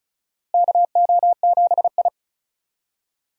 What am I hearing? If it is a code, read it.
Morse code, heard as a keyed tone, about 34 wpm, 700 Hz. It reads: KO7I
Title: USA Call Signs - 4 Characters - Repeated 2x - 35wpm